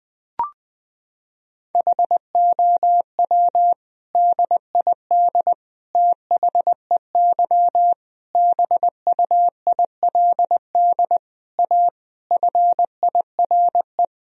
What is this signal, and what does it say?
Morse code, heard as a keyed tone, about 20 wpm, 700 Hz. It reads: HOW DID THEY BUILD A FIRE
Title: Sentences from Top 1000 Words - Review Part 3 - 20wpm